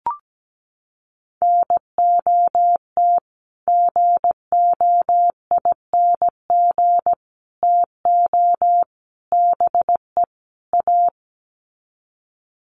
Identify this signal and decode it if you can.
Morse code, heard as a keyed tone, about 17 wpm, 700 Hz. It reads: NOT GOING TO BE A
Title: Sets of 5 Words Encore 17wpm